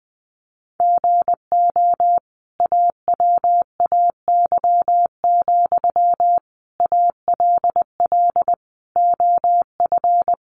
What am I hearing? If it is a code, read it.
Morse code, heard as a keyed tone, about 20 wpm, 700 Hz. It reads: GO AWAY, ALL OF
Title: Sentences from Top 600 Words 20wpm